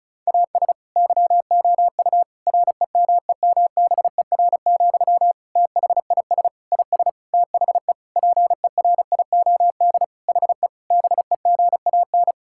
Morse code, heard as a keyed tone, about 36 wpm, 700 Hz. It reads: AS YOU REMEMBER, THIS IS THE PERIOD HE BEGAN
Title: Sentences from Top 1000 Words - Review Part 5 - 35wpm